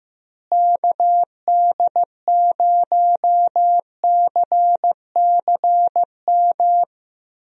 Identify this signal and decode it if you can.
Morse code, heard as a keyed tone, about 15 wpm, 700 Hz. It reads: KD0CCM
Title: USA Call Signs - 6 Characters - Repeated 2x - 15wpm